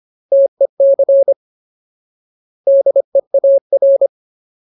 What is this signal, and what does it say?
Morse code, heard as a keyed tone, about 25 wpm, 550 Hz. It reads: TEC DEAR
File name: Ultimate Word List / 25WPM Ultimate